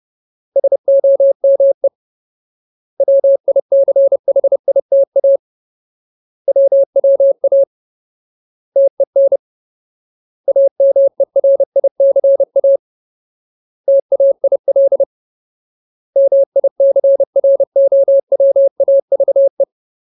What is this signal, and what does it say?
Morse code, heard as a keyed tone, about 30 wpm, 550 Hz. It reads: SOME WICHITA WWA TEN AMERICA TAIL MICROWAVE
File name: Ultimate Word List / 30WPM Ultimate